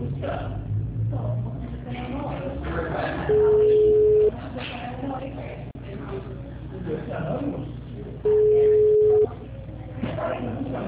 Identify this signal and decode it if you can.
Ringback tone (Europe, 425 Hz), with speech.